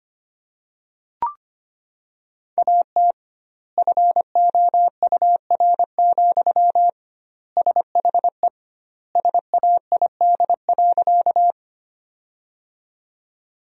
Morse code, heard as a keyed tone, about 25 wpm, 700 Hz. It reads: AT FOUR, SHE SAID.
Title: 2x Word-Spacing - Sentences from Top 300 Words - 25wpm